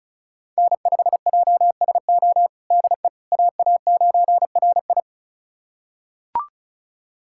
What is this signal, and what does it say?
Morse code, heard as a keyed tone, about 34 wpm, 700 Hz. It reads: N5JSO DE AA9RI
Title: USA Callsign DE Callsign - 5 Characters - 35wpm